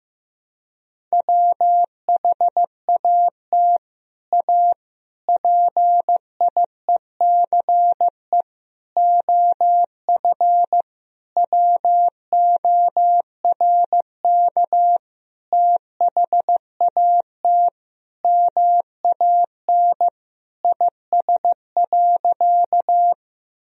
Morse code, heard as a keyed tone, about 16 wpm, 700 Hz. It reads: WHAT A PIECE OF WORK THAT MAN IS.
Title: Sentences from Top 1000 Words - Review Part 2 - 15wpm